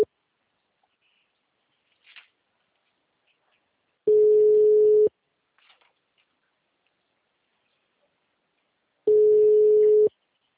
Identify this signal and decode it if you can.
Ringback tone (Europe, 425 Hz).
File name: out- -2014- - - .33627